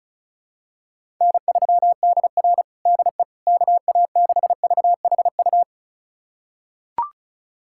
Morse code, heard as a keyed tone, about 34 wpm, 700 Hz. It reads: N3DR DE KA6VHU
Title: USA Callsign DE Callsign 35wpm